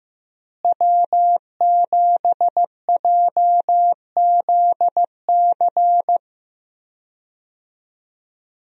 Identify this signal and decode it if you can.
Morse code, heard as a keyed tone, about 15 wpm, 700 Hz. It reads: W7JZC
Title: USA Call Signs - 5 Characters - Repeated 2x - 15wpm